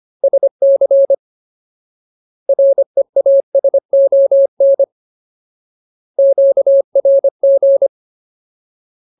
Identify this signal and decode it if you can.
Morse code, heard as a keyed tone, about 25 wpm, 550 Hz. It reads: SC REASON QRG
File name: Ultimate Word List / 25WPM Ultimate